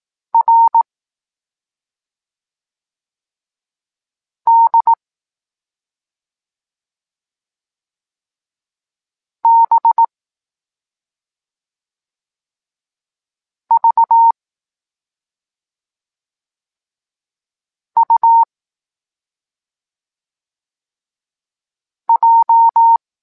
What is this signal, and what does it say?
Morse code, heard as a keyed tone, about 18 wpm, 920 Hz. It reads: R D B V U J